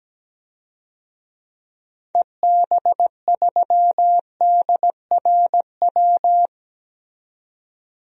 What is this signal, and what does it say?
Morse code, heard as a keyed tone, about 17 wpm, 700 Hz. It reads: EB3DRW
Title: DXCC-100 Call Signs - 6 Characters - 17wpm